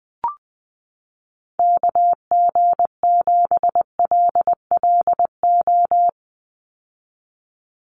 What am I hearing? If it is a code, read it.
Morse code, heard as a keyed tone, about 20 wpm, 700 Hz. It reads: KG7LLO
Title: USA Call Signs 20wpm